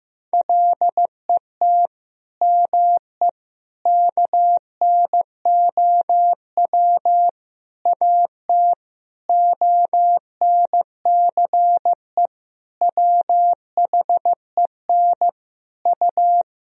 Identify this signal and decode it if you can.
Morse code, heard as a keyed tone, about 15 wpm, 700 Hz. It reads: LET ME KNOW AT ONCE WHEN U
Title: Sentences from Top 300 Words and Abbreviations 15wpm